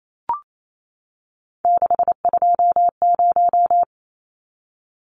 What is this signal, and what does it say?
Morse code, heard as a keyed tone, about 28 wpm, 700 Hz. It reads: 620
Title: Numbers - 3-Digit 28wpm